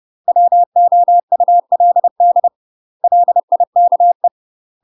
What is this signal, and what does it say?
Morse code, heard as a keyed tone, about 30 wpm, 700 Hz. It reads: WOULD LIKE
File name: UK MASTER QSOs TRAINING / 30WPM / UK MASTER QSOs 0041